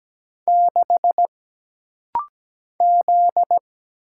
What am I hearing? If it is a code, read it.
Morse code, heard as a keyed tone, about 17 wpm, 700 Hz. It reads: 6 Z
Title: Single Letter-Number - Rapid-Fire 17wpm